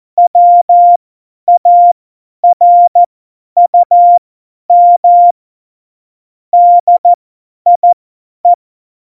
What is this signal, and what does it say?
Morse code, heard as a keyed tone, about 14 wpm, spaced as for 10 wpm, 700 Hz. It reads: WARUM DIE
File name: Morsekurs / Witze / Witz 22. Oktober 2019, 50 F70